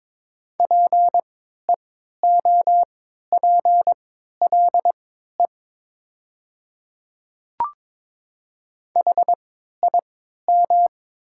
Morse code, heard as a keyed tone, about 22 wpm, spaced as for 12 wpm, 700 Hz. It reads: PEOPLE HIM
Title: 3x Character Spacing - Top 200 Words - 22wpm